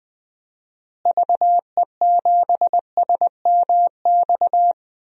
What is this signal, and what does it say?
Morse code, heard as a keyed tone, about 21 wpm, 700 Hz. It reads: VE7SMX
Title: DXCC-100 Call Signs - 6 Characters - Repeated 2x - 20wpm